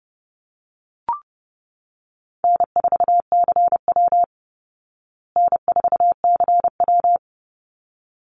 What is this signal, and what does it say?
Morse code, heard as a keyed tone, about 30 wpm, 700 Hz. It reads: N4CW N4CW
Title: Easy Call Signs Repeated 2x - 30wpm